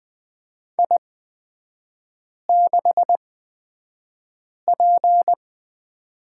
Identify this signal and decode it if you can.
Morse code, heard as a keyed tone, about 20 wpm, spaced as for 5 wpm, 700 Hz. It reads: I6P
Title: 224 Lesson 36e - BT - Three Characters 20wpm